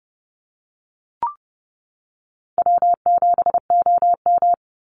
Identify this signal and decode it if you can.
Morse code, heard as a keyed tone, about 30 wpm, 700 Hz. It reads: W7OM
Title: Easy Call Signs - 30wpm